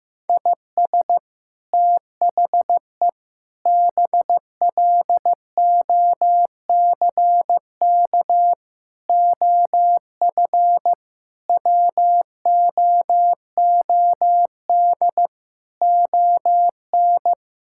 Morse code, heard as a keyed tone, about 15 wpm, 700 Hz. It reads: IS THE BLOCK OF WOOD ON
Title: Sentences from Top 1600 Words 15wpm